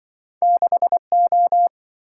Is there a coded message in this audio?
Yes: Morse code from 0.4 to 1.7 s, heard as a keyed tone, about 28 wpm, 700 Hz. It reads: 6O